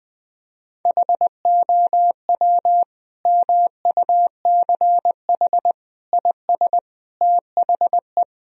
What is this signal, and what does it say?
Morse code, heard as a keyed tone, about 21 wpm, 700 Hz. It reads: HOW MUCH IS THE
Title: Sentences from Top 1000 Words - Review Part 1 - 20wpm